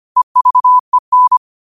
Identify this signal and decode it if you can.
Morse code, heard as a keyed tone, about 25 wpm, 1000 Hz. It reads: EVEN
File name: qotdfiles / 25wpm